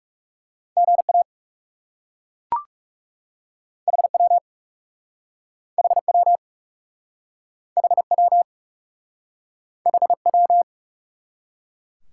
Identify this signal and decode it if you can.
Morse code, heard as a keyed tone, about 36 wpm, 700 Hz. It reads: GA HW HW HW HW
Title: Speed-Racing Top 180 QSO Elements 30wpm